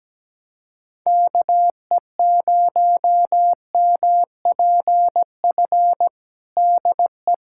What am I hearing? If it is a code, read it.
Morse code, heard as a keyed tone, about 17 wpm, 700 Hz. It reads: KE0MPF DE
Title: Practice QSO - Exchange 4 - Ending - 17wpm